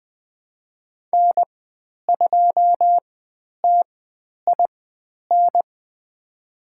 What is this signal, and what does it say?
Morse code, heard as a keyed tone, about 20 wpm, spaced as for 10 wpm, 700 Hz. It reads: N2TIN
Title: 109 Lesson 18g - Letter m - Call Signs 20wpm